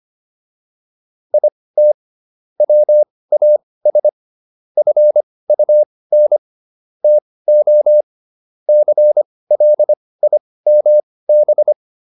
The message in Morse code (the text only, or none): IT WAS FUN TO CLIMB